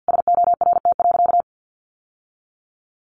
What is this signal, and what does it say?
Morse code, heard as a keyed tone, about 50 wpm, 700 Hz. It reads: SORT.
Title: Sentences from Top 1300 Words 50wpm